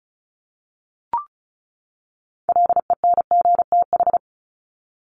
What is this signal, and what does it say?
Morse code, heard as a keyed tone, about 34 wpm, 700 Hz. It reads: LENGTH
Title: Top 751-1000 Words 35wpm